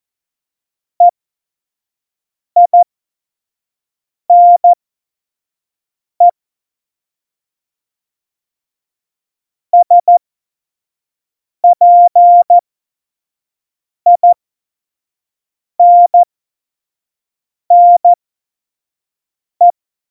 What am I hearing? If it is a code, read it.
Morse code, heard as a keyed tone, about 14 wpm, spaced as for 5 wpm, 700 Hz. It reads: EINE SPINNE
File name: Morsekurs / Witze / Witz 02. Oktober 2018, 25 F70